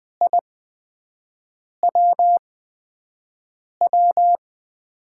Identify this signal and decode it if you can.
Morse code, heard as a keyed tone, about 20 wpm, spaced as for 5 wpm, 700 Hz. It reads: I WW
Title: USA Call Signs - 5 Characters - 8x Character Spacing - 20wpm